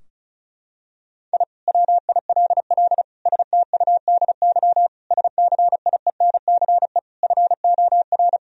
Morse code, heard as a keyed tone, about 34 wpm, 700 Hz. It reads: I WILL STUDY SCIENCE FOR
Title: Sentences from Top 1000 Words - Review Part 2 - 35wpm